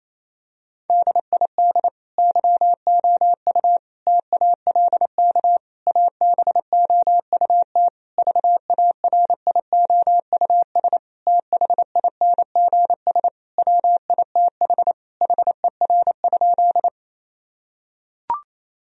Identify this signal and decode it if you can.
Morse code, heard as a keyed tone, about 28 wpm, 700 Hz. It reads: DID YOU TALK ABOUT VARIOUS THINGS WITH HER?
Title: Sentences from Top 1000 Words - Review Part 4 - 28wpm